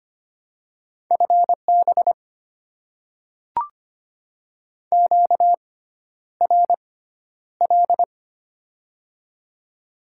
Morse code, heard as a keyed tone, about 25 wpm, spaced as for 11 wpm, 690 Hz. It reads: FB QRL
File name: Top 20 QSO Elements - 6x Character Spacing - 25wpm